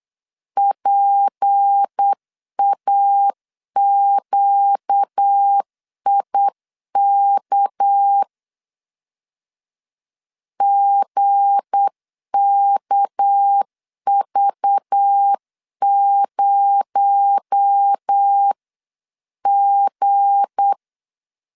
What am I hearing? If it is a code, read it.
Morse code, heard as a keyed tone, about 8 wpm, 800 Hz. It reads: PAQIK GKV0G
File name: LEZIONI / 7 GF9Z / 7 G 8wpm 3118